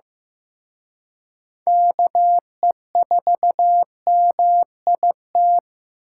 Morse code, heard as a keyed tone, about 18 wpm, 700 Hz. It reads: KE4MIT
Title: Speed Racing - USA 2x3 Call Signs 15wpm